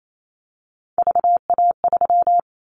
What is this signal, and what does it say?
Morse code, heard as a keyed tone, about 28 wpm, 700 Hz. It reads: VA3
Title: Callsign Prefixes for Top 50 DXCC - 28wpm